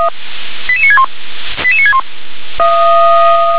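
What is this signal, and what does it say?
Five-tone selective call (ZVEI), heard as quick runs of five tones, tones of about 70 ms. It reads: 88961 88961
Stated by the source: ZVEI five-tone selective call received by SDR.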